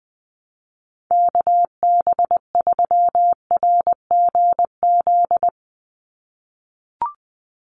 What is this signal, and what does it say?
Morse code, heard as a keyed tone, about 20 wpm, 700 Hz. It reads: KB3RGZ
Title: USA Call Signs 20wpm